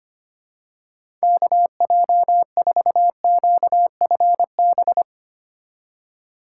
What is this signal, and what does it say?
Morse code, heard as a keyed tone, about 25 wpm, 700 Hz. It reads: KJ4QFB
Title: USA Call Signs - 2x Character Spacing - 25wpm